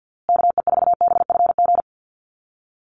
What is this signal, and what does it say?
Morse code, heard as a keyed tone, about 50 wpm, 700 Hz. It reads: KE4BFG
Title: USA Call Signs Repeated 3x 50wpm